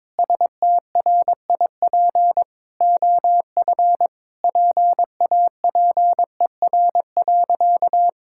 Morse code, heard as a keyed tone, about 22 wpm, 700 Hz. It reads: STRIP OF PAPER.
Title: Sentences from Top 1600 Words 22wpm